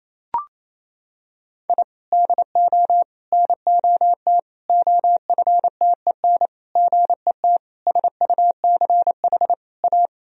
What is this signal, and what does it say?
Morse code, heard as a keyed tone, about 28 wpm, 700 Hz. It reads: I DO NOT OFTEN GET SUCH A
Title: Sentences from Top 1000 Words - Review Part 1 - 28wpm